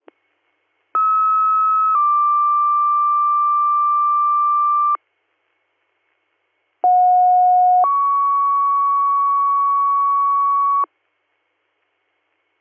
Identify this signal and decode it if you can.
Two-tone page, heard as a one-second tone, then a three-second tone, about 1290 then 1180 Hz, about 730 then 1120 Hz. It is heard from 0.9 to 10.9 s.